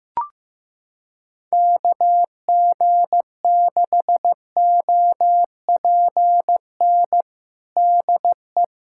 Morse code, heard as a keyed tone, about 15 wpm, 700 Hz. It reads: KG6OPN DE
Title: USA Callsign DE Callsign 15wpm